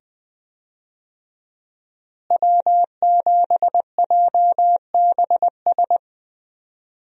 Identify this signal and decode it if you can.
Morse code, heard as a keyed tone, about 20 wpm, 700 Hz. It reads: W7JBS
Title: USA Call Signs - 5 Characters - Repeated 2x - 20wpm